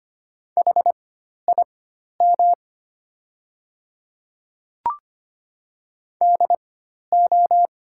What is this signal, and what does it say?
Morse code, heard as a keyed tone, about 25 wpm, spaced as for 12 wpm, 700 Hz. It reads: HIM DO
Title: 4x Character Spacing - Top 100 Words - 25wpm